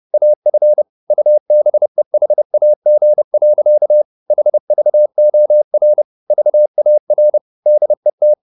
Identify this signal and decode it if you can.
Morse code, heard as a keyed tone, about 30 wpm, 600 Hz. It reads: AF UBEHAG. HVOR VAR DET
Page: Ferie 30 wpm